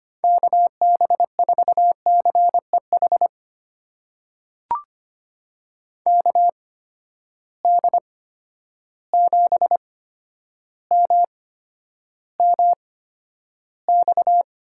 Morse code, heard as a keyed tone, about 25 wpm, spaced as for 11 wpm, 700 Hz. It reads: KB4CEH KD7MMX
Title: USA Call Signs - 6 Characters - 8x Character Spacing - 25wpm